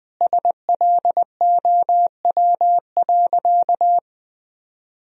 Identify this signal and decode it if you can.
Morse code, heard as a keyed tone, about 21 wpm, 700 Hz. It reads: SLOW.
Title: Sentences from Top 1000 Words - Review Part 2 - 20wpm